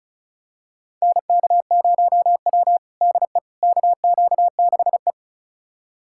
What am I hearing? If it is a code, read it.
Morse code, heard as a keyed tone, about 36 wpm, 700 Hz. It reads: NK0W DE KQ6E
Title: USA Callsign DE Callsign - 4 Characters - 35wpm